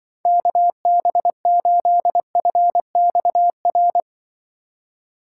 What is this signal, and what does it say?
Morse code, heard as a keyed tone, about 25 wpm, 700 Hz. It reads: KB8FXR